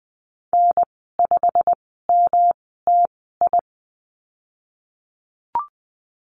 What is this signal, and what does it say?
Morse code, heard as a keyed tone, about 20 wpm, spaced as for 15 wpm, 700 Hz. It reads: N5MTI
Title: 140 Lesson 23g - Letter f - Call Signs 20wpm